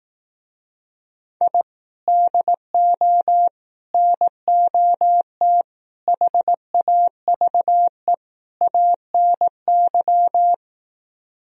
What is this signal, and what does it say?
Morse code, heard as a keyed tone, about 18 wpm, 700 Hz. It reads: I DO NOT HAVE ANY
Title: Speed-Racing - Sets of 5 Words 15wpm